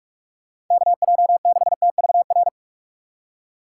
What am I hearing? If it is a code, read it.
Morse code, heard as a keyed tone, about 46 wpm, 700 Hz. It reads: KJ6TVR
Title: USA Call Signs - 6 Characters - 4x Character Spacing - 45wpm